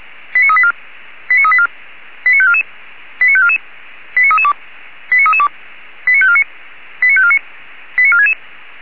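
Five-tone selective call (ZVEI), heard as quick runs of five tones, tones of about 80 ms. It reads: 89384 89384 89544 89544 89332 89332 89549 89549 89466
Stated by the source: ZVEI five-tone call 89386, received by SDR.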